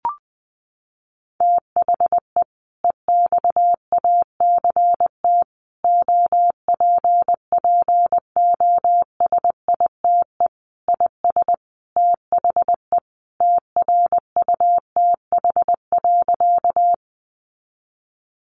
Morse code, heard as a keyed tone, about 20 wpm, 700 Hz. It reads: THE EXACT OPPOSITE IS THE TRUTH.
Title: Sentences from Top 1300 Words 20wpm